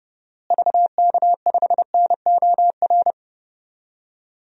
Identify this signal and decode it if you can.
Morse code, heard as a keyed tone, about 32 wpm, 700 Hz. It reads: VK5NOR
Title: DXCC-100 Call Signs - 6 Characters - Repeated 3x - 30wpm